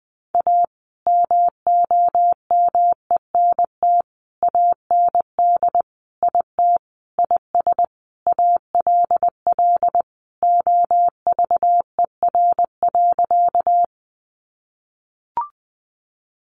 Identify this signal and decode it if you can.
Morse code, heard as a keyed tone, about 20 wpm, 700 Hz. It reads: A MOMENT AND IT IS ALL OVER.
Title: Sentences from Top 1000 Words 20wpm